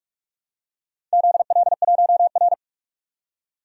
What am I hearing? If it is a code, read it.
Morse code, heard as a keyed tone, about 44 wpm, 700 Hz. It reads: ZL1R
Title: DXCC-100 Call Signs - 4 Characters - 45wpm